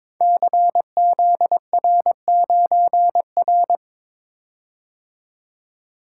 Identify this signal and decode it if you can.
Morse code, heard as a keyed tone, about 22 wpm, 700 Hz. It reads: CZR9R
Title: Code Groups - 5 Letter:Numbers - 22wpm